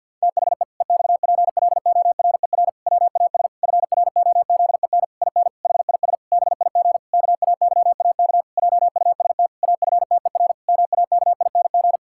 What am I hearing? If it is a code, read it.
Morse code, heard as a keyed tone, about 50 wpm, 700 Hz. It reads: THE EXPLORER WAS FROZEN IN HIS BIG KAYAK JUST AFTER MAKING
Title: Pangram Challenge v2 - 50wpm